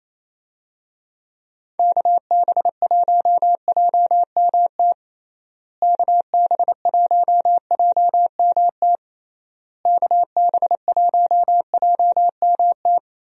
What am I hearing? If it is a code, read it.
Morse code, heard as a keyed tone, about 27 wpm, 700 Hz. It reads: KB1JMT KB1JMT KB1JMT
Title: USA Call Signs - 6 Characters - Repeated 3x - 28wpm